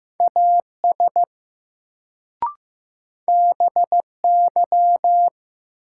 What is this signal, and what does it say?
Morse code, heard as a keyed tone, about 15 wpm, 700 Hz. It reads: AS BY
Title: Sentences from Top 100 Words - New Word Drill - 15wpm